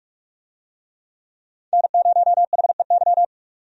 Morse code, heard as a keyed tone, about 44 wpm, 700 Hz. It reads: N0HEY
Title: USA Call Signs - 5 Characters - 45wpm